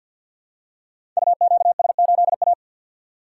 Morse code, heard as a keyed tone, about 47 wpm, 700 Hz. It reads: UQS8A
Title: Code Groups - 5 Letter:Numbers - 50wpm